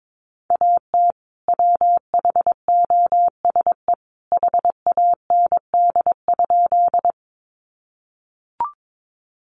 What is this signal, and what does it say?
Morse code, heard as a keyed tone, about 22 wpm, 700 Hz. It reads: AT WHOSE HAND?
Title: Sentences from Top 1200 Words 22wpm